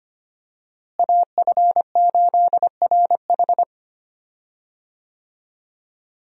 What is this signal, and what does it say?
Morse code, heard as a keyed tone, about 25 wpm, 700 Hz. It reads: AF8RH
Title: USA Call Signs - 5 Characters - Repeated 2x - 25wpm